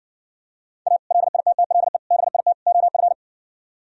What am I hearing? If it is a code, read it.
Morse code, heard as a keyed tone, about 60 wpm, 700 Hz. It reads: A LITTLE BIT OF